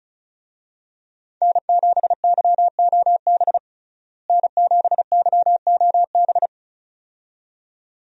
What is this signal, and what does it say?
Morse code, heard as a keyed tone, about 34 wpm, 700 Hz. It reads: N7YOB N7YOB
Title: USA Call Signs - 5 Characters - Repeated 2x - 35wpm